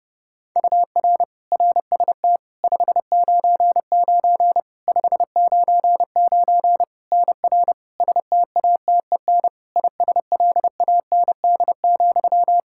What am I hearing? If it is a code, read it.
Morse code, heard as a keyed tone, about 32 wpm, 700 Hz. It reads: UR RST 599 599 NR STATEN ISLAND,
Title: Practice QSO - Exchange 1 - Beginning - 30wpm